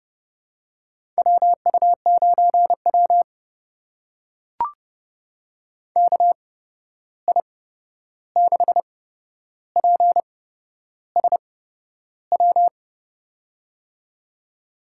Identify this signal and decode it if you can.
Morse code, heard as a keyed tone, about 30 wpm, spaced as for 11 wpm, 700 Hz. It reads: WU9W KI6PSW
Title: USA Call Signs - 8x Character Spacing - 30wpm